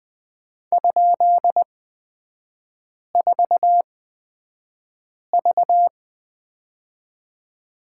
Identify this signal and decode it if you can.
Morse code, heard as a keyed tone, about 20 wpm, spaced as for 5 wpm, 700 Hz. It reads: ?4V
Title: 224 Lesson 36e - BT - Three Characters 20wpm